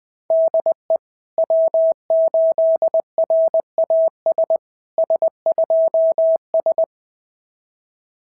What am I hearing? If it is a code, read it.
Morse code, heard as a keyed tone, about 20 wpm, 690 Hz. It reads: DE W8RAS S2S
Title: SOTA QSO Practice 20wpm